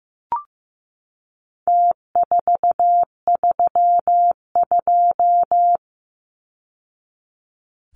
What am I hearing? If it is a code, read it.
Morse code, heard as a keyed tone, about 15 wpm, 700 Hz. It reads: T432
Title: Numbers - Contest Style - 4-Digit 15wpm